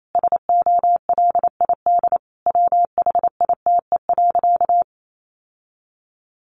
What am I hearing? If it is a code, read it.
Morse code, heard as a keyed tone, about 28 wpm, 700 Hz. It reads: SOLID WHITE.
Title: Sentences from Top 1500 Words 28wpm